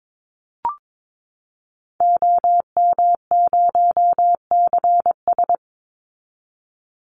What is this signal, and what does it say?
Morse code, heard as a keyed tone, about 22 wpm, 700 Hz. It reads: OM0CS
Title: Call Signs Repeated 3x - 22wpm